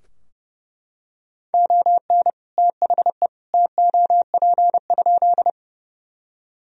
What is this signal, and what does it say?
Morse code, heard as a keyed tone, about 31 wpm, 700 Hz. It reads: ON THE TOP?
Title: Sentences from Top 1000 Words - Review Part 2 - 30wpm